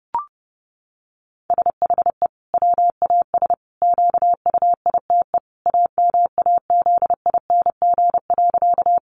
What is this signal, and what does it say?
Morse code, heard as a keyed tone, about 30 wpm, 700 Hz. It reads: SHE WAS QUITE AMAZING.
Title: Sentences from Top 1500 Words 30wpm